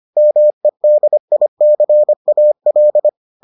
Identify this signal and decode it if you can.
Morse code, heard as a keyed tone, about 25 wpm, 600 Hz. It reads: MEDICAL